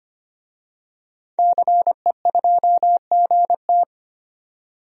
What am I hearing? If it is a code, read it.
Morse code, heard as a keyed tone, about 25 wpm, 700 Hz. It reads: CE2GT
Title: DXCC-100 Call Signs - 5 Characters - 25wpm